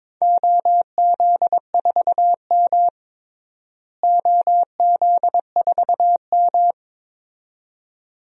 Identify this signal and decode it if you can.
Morse code, heard as a keyed tone, about 22 wpm, 700 Hz. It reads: OZ4M OZ4M
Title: DXCC-100 Call Signs - 4 Characters - Repeated 2x - 22wpm